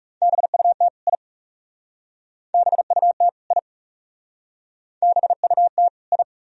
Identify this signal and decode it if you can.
Morse code, heard as a keyed tone, about 38 wpm, 700 Hz. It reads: BUT I BUT I BUT I